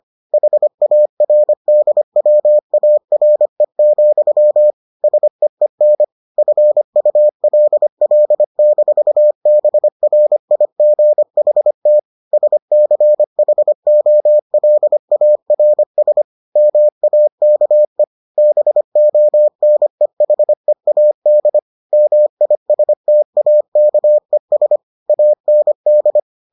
Morse code, heard as a keyed tone, about 25 wpm, 600 Hz. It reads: HARDWARE, SEEN FULL-BRIGHT SCHOLARS MAKE BONEHEAD MISTAKES AND